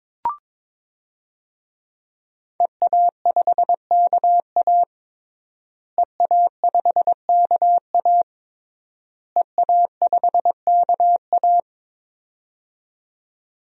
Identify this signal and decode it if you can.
Morse code, heard as a keyed tone, about 22 wpm, 700 Hz. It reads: EA5KA EA5KA EA5KA
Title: DXCC-100 Call Signs - 5 Characters - Repeated 3x - 22wpm